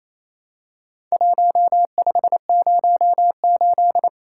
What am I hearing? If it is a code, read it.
Morse code, heard as a keyed tone, about 27 wpm, 700 Hz. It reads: 1508
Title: Worked All Europe DX Contest - QTC Exchange - 28wpm